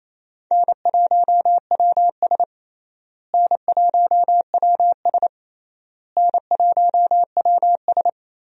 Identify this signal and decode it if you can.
Morse code, heard as a keyed tone, about 28 wpm, 700 Hz. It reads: N1WS N1WS N1WS
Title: USA Call Signs - 4 Characters - Repeated 3x - 28wpm